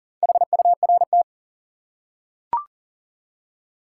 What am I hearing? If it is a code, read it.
Morse code, heard as a keyed tone, about 40 wpm, 700 Hz. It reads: HURT
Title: Four Letter Words - 40wpm